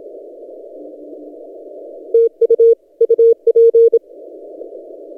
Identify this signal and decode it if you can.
Morse code, heard as a keyed tone, about 27 wpm, 460 Hz. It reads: TU UP
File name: posts / t32rc-30cw